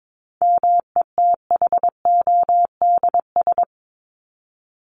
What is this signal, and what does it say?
Morse code, heard as a keyed tone, about 22 wpm, 700 Hz. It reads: METHODS
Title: ES:S Suffix - Top 100 Words with Base Words 22wpm